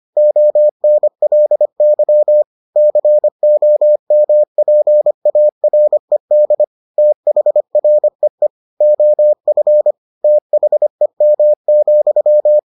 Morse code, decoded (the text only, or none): ONLY COMPARED THREE OF THEM,